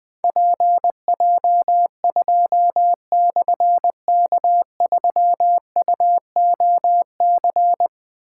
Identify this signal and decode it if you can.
Morse code, heard as a keyed tone, about 20 wpm, 700 Hz. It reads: PJ2/K3UOC
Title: Difficult Call Signs 20wpm